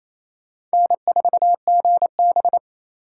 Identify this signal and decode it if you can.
Morse code, heard as a keyed tone, about 34 wpm, 700 Hz. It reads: N4GB
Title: Speed Racing - USA 1x2 Call Signs 28wpm